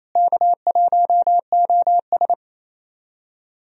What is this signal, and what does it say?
Morse code, heard as a keyed tone, about 28 wpm, 700 Hz. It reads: K1OS
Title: USA Call Signs - 4 Characters - Repeated 3x - 28wpm